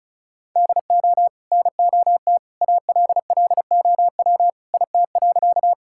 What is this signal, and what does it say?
Morse code, heard as a keyed tone, about 34 wpm, 700 Hz. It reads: DO NOT ALLOW IT.
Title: Sentences from Top 1000 Words - Review Part 5 - 35wpm